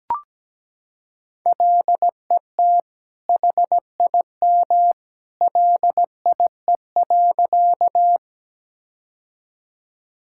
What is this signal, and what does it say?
Morse code, heard as a keyed tone, about 17 wpm, 700 Hz. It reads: LET HIM LIE.
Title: Sentences from Top 1600 Words 17wpm